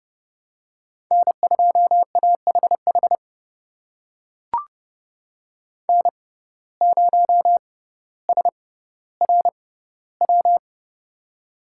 Morse code, heard as a keyed tone, about 31 wpm, spaced as for 16 wpm, 700 Hz. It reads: N2AHH N0SRW
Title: USA Call Signs - 5 Characters - 6x Character Spacing - 30wpm